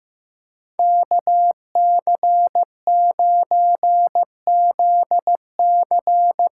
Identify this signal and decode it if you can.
Morse code, heard as a keyed tone, about 16 wpm, 700 Hz. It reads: KC9ZC
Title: USA Call Signs - 6 Characters - 8x Character Spacing - 15wpm